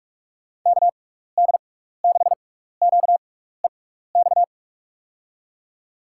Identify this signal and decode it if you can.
Morse code, heard as a keyed tone, about 42 wpm, spaced as for 15 wpm, 700 Hz. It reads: KD6QEX
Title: USA Call Signs - 6x Character Spacing - 45wpm